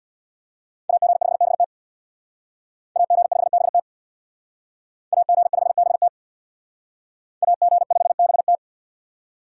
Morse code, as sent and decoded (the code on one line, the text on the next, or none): .- --. ..... -... - / .- --. ..... -... - / .- --. ..... -... - / .- --. ..... -... -
AG5BT AG5BT AG5BT AG5BT